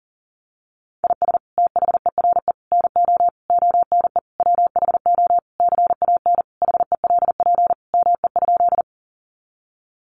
Morse code, heard as a keyed tone, about 37 wpm, 700 Hz. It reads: IS THERE NO ONE WHO CAN HELP ME?
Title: Sentences from Top 200 Words 40wpm